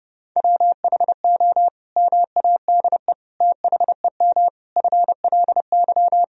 Morse code, heard as a keyed tone, about 30 wpm, 700 Hz. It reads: WHO MADE THEM FLY
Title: Sentences from Top 1000 Words - Review Part 2 - 30wpm